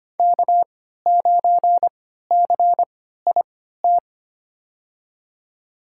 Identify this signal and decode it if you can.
Morse code, heard as a keyed tone, about 25 wpm, spaced as for 14 wpm, 700 Hz. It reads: K9CIT
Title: USA Call Signs - 5 Characters - 3x Character Spacing - 25wpm